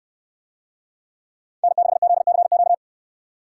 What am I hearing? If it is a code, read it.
Morse code, heard as a keyed tone, about 66 wpm, 700 Hz. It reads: N6ZCQ